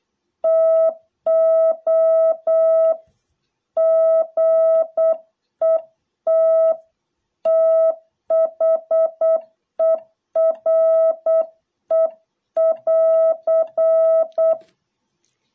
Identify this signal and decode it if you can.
Morse code, heard as a keyed tone, about 8 wpm, 660 Hz. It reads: TO GET THERE<AR>